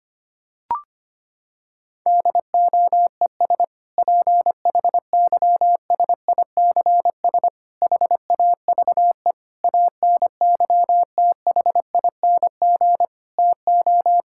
Morse code, heard as a keyed tone, about 25 wpm, 700 Hz. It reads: DOES PHYSICS HAVE ANYTHING TO
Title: Sentences from Top 1000 Words - Review Part 5 - 25wpm